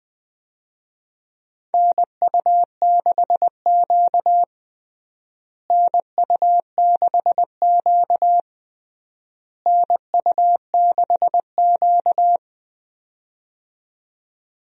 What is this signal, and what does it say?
Morse code, heard as a keyed tone, about 20 wpm, 700 Hz. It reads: NU6Q NU6Q NU6Q
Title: USA Call Signs - 4 Characters - Repeated 3x - 20wpm